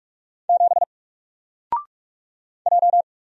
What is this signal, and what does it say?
Morse code, heard as a keyed tone, about 48 wpm, 700 Hz. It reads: 7 J
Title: Single Character (Letter:Number:Punctuation:Prosign) Alt 45wpm